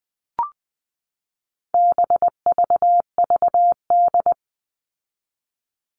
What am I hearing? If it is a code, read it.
Morse code, heard as a keyed tone, about 20 wpm, 700 Hz. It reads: BVVD
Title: All Cut Numbers - 4-Digit 20wpm